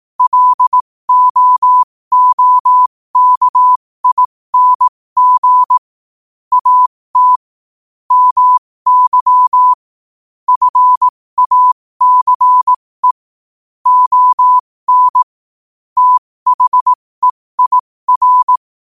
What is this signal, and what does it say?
Morse code, heard as a keyed tone, about 18 wpm, spaced as for 15 wpm, 1000 Hz. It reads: LOOKING AT MY FACE ON THEIR
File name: 15wpm